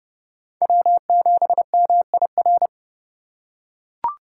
Morse code, heard as a keyed tone, about 29 wpm, 700 Hz. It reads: W7MIR